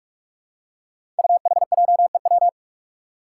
Morse code, heard as a keyed tone, about 46 wpm, 700 Hz. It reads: UHJEW
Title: Five Challenging Letter-Number Combinations 45wpm